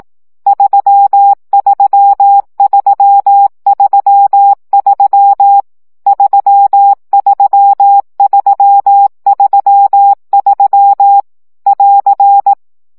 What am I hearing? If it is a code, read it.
Morse code, heard as a keyed tone, about 18 wpm, 800 Hz. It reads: 33333 33333 <AR>
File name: CW gyakorlas 3 15 baud